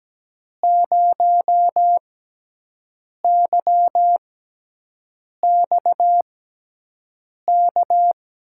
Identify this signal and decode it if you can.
Morse code, heard as a keyed tone, about 17 wpm, spaced as for 6 wpm, 700 Hz. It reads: 0YXK
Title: USA Call Signs - 6 Characters - 6x Character Spacing - 17wpm